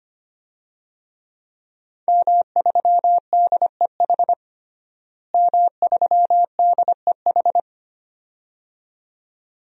Morse code, heard as a keyed tone, about 25 wpm, 700 Hz. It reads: M3DEH M3DEH
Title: DXCC-100 Call Signs - 5 Characters - Repeated 2x - 25wpm